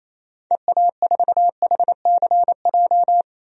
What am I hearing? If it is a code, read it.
Morse code, heard as a keyed tone, about 27 wpm, 700 Hz. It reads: EA4HCJ
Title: DXCC-100 Call Signs - 6 Characters - Repeated 2x - 28wpm